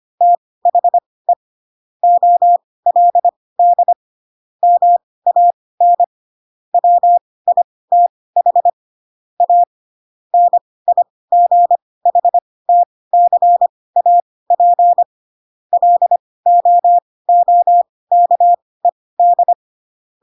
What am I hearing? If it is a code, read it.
Morse code, heard as a keyed tone, about 25 wpm, spaced as for 18 wpm, 700 Hz. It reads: THE OLD MAN WITH A NIGHTCAP LOOKED
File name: Pinocchio / Pinocchio ch7 12 25 ik1hge